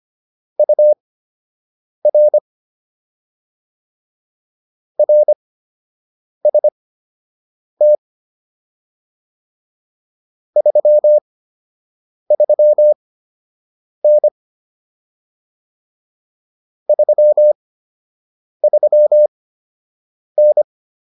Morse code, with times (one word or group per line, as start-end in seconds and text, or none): 0.6-2.4 UR
5.0-8.0 RST
10.6-14.3 33N
16.9-20.6 33N